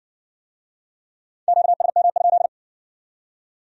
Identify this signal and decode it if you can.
Morse code, heard as a keyed tone, about 53 wpm, 700 Hz. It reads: QSM?